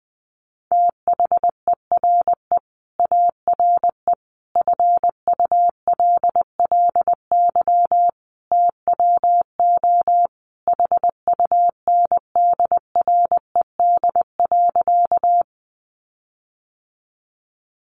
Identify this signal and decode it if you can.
Morse code, heard as a keyed tone, about 20 wpm, 700 Hz. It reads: THERE ARE FULLY TWO HUNDRED.
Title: Sentences from Top 1300 Words 20wpm